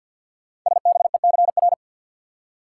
Morse code, heard as a keyed tone, about 48 wpm, 690 Hz. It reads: SBECR
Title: Code Groups - 5 Letters - 50wpm